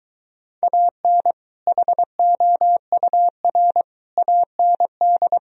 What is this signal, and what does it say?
Morse code, heard as a keyed tone, about 24 wpm, 700 Hz. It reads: AN HOUR AND